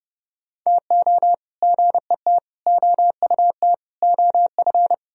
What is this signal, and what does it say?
Morse code, heard as a keyed tone, about 38 wpm, 700 Hz. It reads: TO GET OUT OF